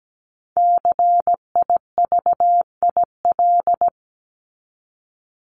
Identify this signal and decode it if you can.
Morse code, heard as a keyed tone, about 17 wpm, 700 Hz. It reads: CIVIL
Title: Top 1751-2000 Words 17wpm